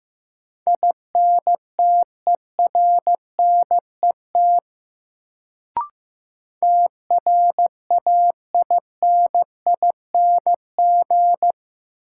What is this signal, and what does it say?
Morse code, heard as a keyed tone, about 15 wpm, 700 Hz. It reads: INTERNET TRAINING
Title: Sentences from Top 1000 Words - Review Part 4 - New Word Drill - 15wpm